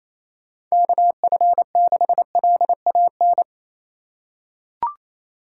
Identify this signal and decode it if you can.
Morse code, heard as a keyed tone, about 28 wpm, 700 Hz. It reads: KF6LAN
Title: USA Call Signs - 6 Characters - 3x Character Spacing - 28wpm